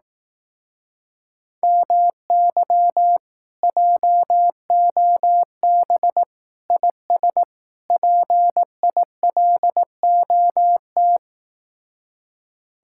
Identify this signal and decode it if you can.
Morse code, heard as a keyed tone, about 18 wpm, 700 Hz. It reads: MY JOB IS PILOT
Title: Speed-Racing Common QSO Phrases Encore - 15wpm